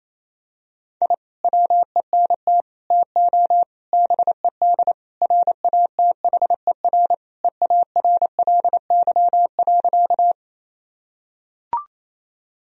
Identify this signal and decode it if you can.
Morse code, heard as a keyed tone, about 28 wpm, 700 Hz. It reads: I WENT TO BED RATHER EARLY.
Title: Sentences from Top 1000 Words - Review Part 4 - 28wpm